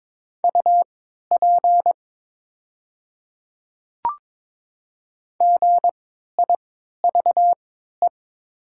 Morse code, heard as a keyed tone, about 22 wpm, spaced as for 13 wpm, 700 Hz. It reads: UP GIVE
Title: 3x Character Spacing - Top 200 Words - 22wpm